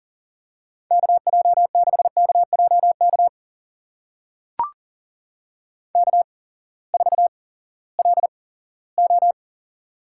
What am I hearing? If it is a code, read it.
Morse code, heard as a keyed tone, about 37 wpm, spaced as for 19 wpm, 700 Hz. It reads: KJ6KJK K4LO
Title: USA Call Signs - 8x Character Spacing - 40wpm